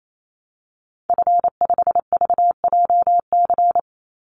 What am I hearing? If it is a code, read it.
Morse code, heard as a keyed tone, about 28 wpm, 700 Hz. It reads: F5VJC
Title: Call Signs Repeated 3x - 28wpm